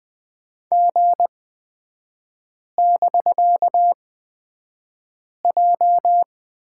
Morse code, heard as a keyed tone, about 20 wpm, spaced as for 5 wpm, 700 Hz. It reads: G<BK>J
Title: 255 Lesson 41e - Period - Three Characters 20wpm